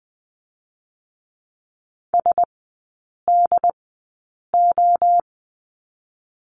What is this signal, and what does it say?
Morse code, heard as a keyed tone, about 20 wpm, spaced as for 8 wpm, 700 Hz. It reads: SDO
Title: 087 Lesson 15e - Number 5 - Three Characters 20wpm